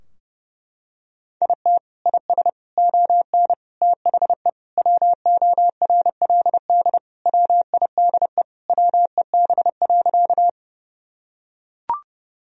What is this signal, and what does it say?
Morse code, heard as a keyed tone, about 30 wpm, 700 Hz. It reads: IT IS ON THE WORLD WIDE WEB.
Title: Sentences from Top 1000 Words - Review Part 4 - 30wpm